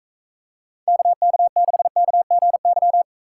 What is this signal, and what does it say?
Morse code, heard as a keyed tone, about 45 wpm, 700 Hz. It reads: KK6KGY